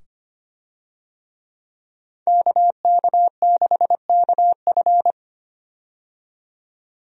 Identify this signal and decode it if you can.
Morse code, heard as a keyed tone, about 25 wpm, 700 Hz. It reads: KK6KF
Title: USA Call Signs - 5 Characters - 25wpm